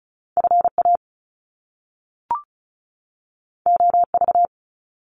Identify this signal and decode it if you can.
Morse code, heard as a keyed tone, about 36 wpm, 700 Hz. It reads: FA OV
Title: Top Two Letter Combinations in Words 35wpm